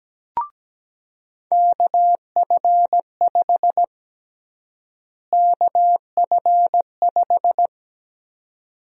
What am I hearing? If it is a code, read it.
Morse code, heard as a keyed tone, about 17 wpm, 700 Hz. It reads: KF5 KF5
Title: USA Call Sign Prefixes - Repeated 2x - 17wpm